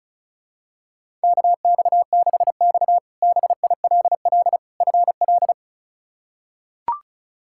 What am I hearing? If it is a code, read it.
Morse code, heard as a keyed tone, about 34 wpm, 700 Hz. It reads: KX6X BILL FL
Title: K1USN SST QSO Practice - 35wpm